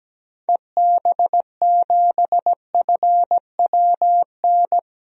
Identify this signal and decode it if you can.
Morse code, heard as a keyed tone, about 17 wpm, 700 Hz. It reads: EB7FWN
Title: DXCC-100 Call Signs - 6 Characters - Repeated 2x - 17wpm